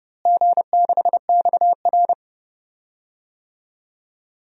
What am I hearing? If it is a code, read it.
Morse code, heard as a keyed tone, about 30 wpm, 700 Hz. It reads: G6XR
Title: DXCC-100 Call Signs - 4 Characters - Repeated 3x - 30wpm